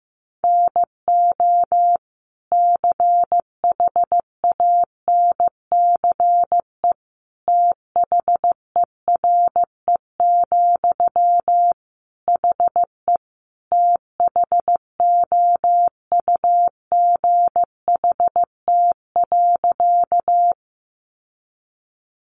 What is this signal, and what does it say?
Morse code, heard as a keyed tone, about 15 wpm, 700 Hz. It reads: NO CHANCE THERE, HE THOUGHT.
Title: Sentences from Top 800 Words 15wpm